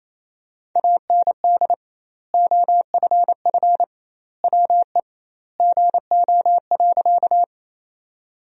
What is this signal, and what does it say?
Morse code, heard as a keyed tone, about 28 wpm, 700 Hz. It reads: AND OFF WE GO.
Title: 2x Word-Spacing - Sentences from Top 200 Words - 28wpm